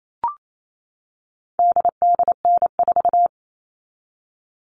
Morse code, heard as a keyed tone, about 28 wpm, 700 Hz. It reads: DDN4
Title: All Cut Numbers - 4-Digit 28wpm